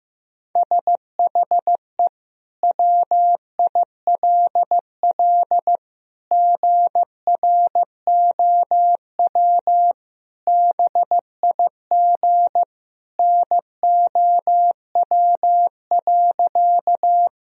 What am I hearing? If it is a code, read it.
Morse code, heard as a keyed tone, about 15 wpm, 700 Hz. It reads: SHE WILL GROW BIG NOW.
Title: Sentences from Top 1000 Words - Review Part 1 - 15wpm